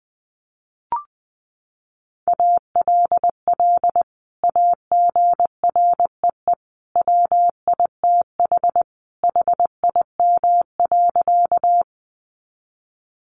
Morse code, heard as a keyed tone, about 20 wpm, 700 Hz. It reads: ALL AGREE WITH HIM.
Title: Sentences from Top 1100 Words 20wpm